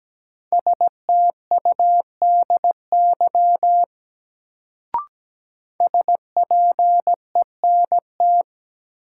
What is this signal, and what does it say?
Morse code, heard as a keyed tone, about 17 wpm, 700 Hz. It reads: STUDY SPENT
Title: Sets of 5 Words - New Word Drill - 17wpm